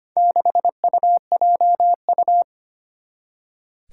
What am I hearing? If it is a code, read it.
Morse code, heard as a keyed tone, about 25 wpm, 700 Hz. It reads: 6UJU
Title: Four Challenging Letter-Number Combinations 25wpm